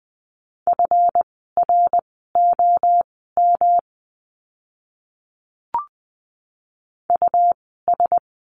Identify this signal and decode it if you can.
Morse code, heard as a keyed tone, about 20 wpm, spaced as for 14 wpm, 700 Hz. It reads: FROM US
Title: 206 Lesson 33f - Letter j - Words 20wpm